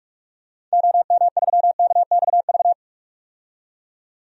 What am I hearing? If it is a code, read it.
Morse code, heard as a keyed tone, about 46 wpm, 700 Hz. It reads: OM3KXV
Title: DXCC-100 Call Signs - 6 Characters - 45wpm